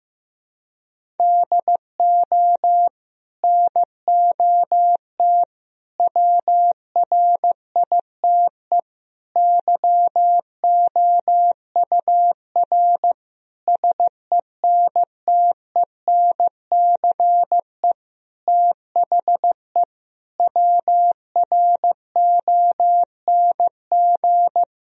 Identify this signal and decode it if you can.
Morse code, heard as a keyed tone, about 15 wpm, 700 Hz. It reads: DO NOT WRITE YOUR SENTENCE THE WRONG
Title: Sentences from Top 1000 Words - Review Part 1 - 15wpm